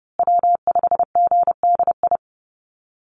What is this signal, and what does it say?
Morse code, heard as a keyed tone, about 30 wpm, 700 Hz. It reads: W5GDI
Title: USA Call Signs Repeated 2x 30wpm